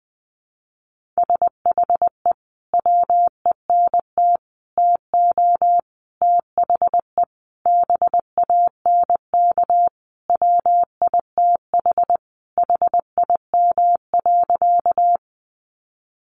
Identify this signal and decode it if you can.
Morse code, heard as a keyed tone, about 20 wpm, 700 Hz. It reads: SHE WENT TO THE BANK WITH HIM.
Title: Sentences from Top 900 Words 20wpm